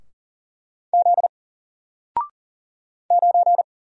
Morse code, heard as a keyed tone, about 38 wpm, 700 Hz. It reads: Z 9
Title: Single Character (Letter:Number:Punctuation:Prosign) 40wpm